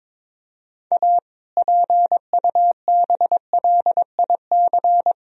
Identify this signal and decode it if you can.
Morse code, heard as a keyed tone, about 22 wpm, 700 Hz. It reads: A PUBLIC
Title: Sentences from Top 1000 Words - Review Part 3 - 22wpm